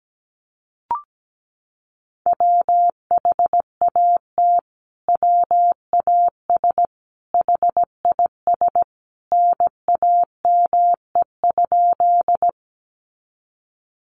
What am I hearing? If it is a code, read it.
Morse code, heard as a keyed tone, about 17 wpm, 700 Hz. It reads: WHAT WAS HIS NAME?
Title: Sentences from Top 200 Words 17wpm